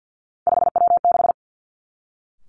Morse code, heard as a keyed tone, about 50 wpm, 700 Hz. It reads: 5W6
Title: Three Challenging Letter-Number Combinations 50wpm